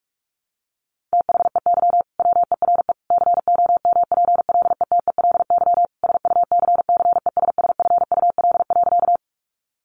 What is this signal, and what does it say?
Morse code, heard as a keyed tone, about 44 wpm, 700 Hz. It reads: THEY WERE COMPLETELY SUCCESSFUL.
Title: Sentences from Top 1100 Words 45wpm